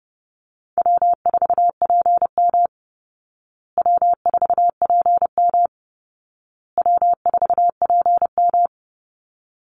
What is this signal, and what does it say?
Morse code, heard as a keyed tone, about 30 wpm, 700 Hz. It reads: W4PM W4PM W4PM
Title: Easy Call Signs Repeated 3x - 30wpm